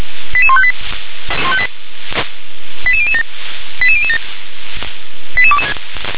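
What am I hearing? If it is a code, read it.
Five-tone selective call (ZVEI), heard as quick runs of five tones, tones of about 70 ms. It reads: 88158 88007 88007
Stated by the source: ZVEI five-tone selective call 88169 received on an SDR.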